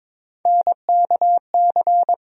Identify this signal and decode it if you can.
Morse code, heard as a keyed tone, about 22 wpm, 700 Hz. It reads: NKC
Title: USA Call Sign Postfixes - 3 Letters 22wpm